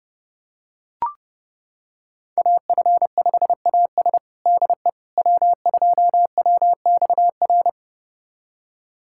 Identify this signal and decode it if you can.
Morse code, heard as a keyed tone, about 30 wpm, 700 Hz. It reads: AF5AS DE W2WXR
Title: USA Callsign DE Callsign - 5 Characters - 30wpm